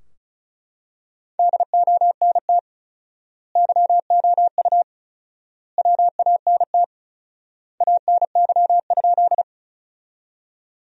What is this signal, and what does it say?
Morse code, heard as a keyed tone, about 36 wpm, 700 Hz. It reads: DONT YOU WANT ANY?
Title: 4x Word-Spacing - Sentences from Top 300 Words - 35wpm